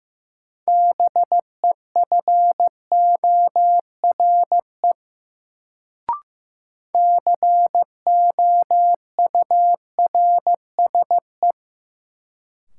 Morse code, heard as a keyed tone, about 15 wpm, 700 Hz. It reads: BEFORE COURSE
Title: Sets of 2 Words - New Word Drill - 15wpm